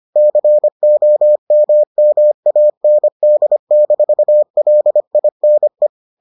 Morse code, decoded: COMMAND-LINE

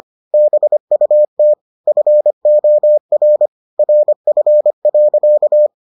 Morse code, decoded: BUT FOR RF.

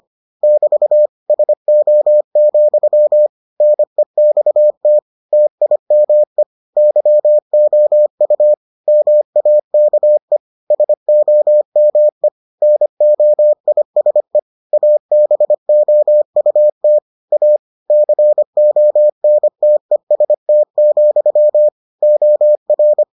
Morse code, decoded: <BT> SO, NEXT TIME YOU MAKE SOME NOISE ABOUT A CONTEST, OR

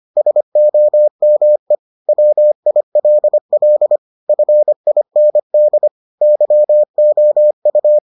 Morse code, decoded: SOME WILL FIND YOU

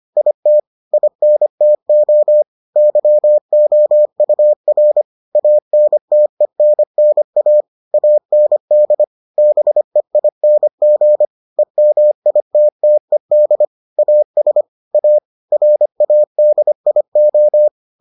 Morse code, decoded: IT INTO YOUR ANTENNA AND BEING EMITTED AS A RADIO

